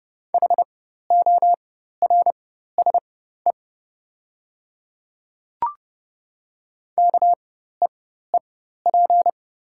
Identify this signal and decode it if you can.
Morse code, heard as a keyed tone, about 30 wpm, spaced as for 14 wpm, 700 Hz. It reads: HORSE KEEP
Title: 4x Character Spacing - Top 300 Words - 30wpm